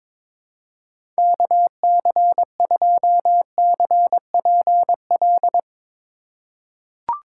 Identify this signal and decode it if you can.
Morse code, heard as a keyed tone, about 22 wpm, 700 Hz. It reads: KC2CPL
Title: USA Call Signs - 6 Characters - 3x Character Spacing - 22wpm